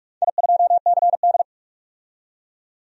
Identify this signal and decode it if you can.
Morse code, heard as a keyed tone, about 44 wpm, 690 Hz. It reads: I2CD
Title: DXCC-100 Call Signs - 4 Characters - 45wpm